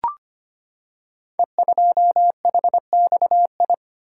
Morse code, heard as a keyed tone, about 25 wpm, 690 Hz. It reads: E2HXI
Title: Code Groups - 5 Letter:Numbers - 25wpm